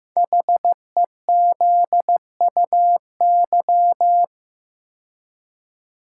Morse code, heard as a keyed tone, about 15 wpm, 700 Hz. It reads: HEZUY
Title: Code Groups - 5 Letters - 15wpm